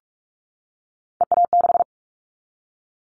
Morse code, heard as a keyed tone, about 44 wpm, 700 Hz. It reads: EA6
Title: Callsign Prefixes for Top 100 DXCC - 45wpm